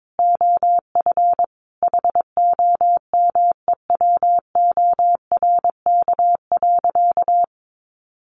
Morse code, heard as a keyed tone, about 22 wpm, 700 Hz. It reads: OF HOMEWORK.
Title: Additional Easy Phrases 22wpm